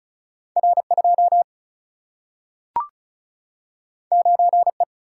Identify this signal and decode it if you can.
Morse code, heard as a keyed tone, about 36 wpm, 700 Hz. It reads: R 2 9 E
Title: Single-Letter Single-Number 35wpm